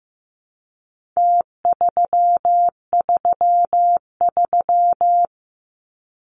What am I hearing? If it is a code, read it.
Morse code, heard as a keyed tone, about 15 wpm, 700 Hz. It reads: T333
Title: Numbers - Contest Style - 4-Digit 15wpm